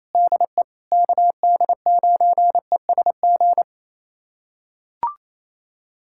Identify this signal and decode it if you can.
Morse code, heard as a keyed tone, about 28 wpm, 700 Hz. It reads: DE KD9ESG
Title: USA Callsign DE Callsign 28wpm